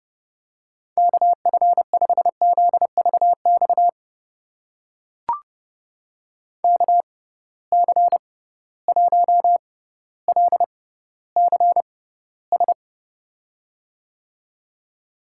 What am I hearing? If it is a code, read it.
Morse code, heard as a keyed tone, about 30 wpm, spaced as for 15 wpm, 700 Hz. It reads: KF5ZVX KC1LCS
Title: USA Call Signs - 6 Characters - 6x Character Spacing - 30wpm